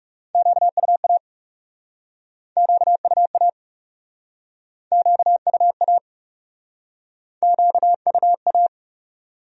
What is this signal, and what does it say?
Morse code, heard as a keyed tone, about 40 wpm, 700 Hz. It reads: QUA QUA QUA QUA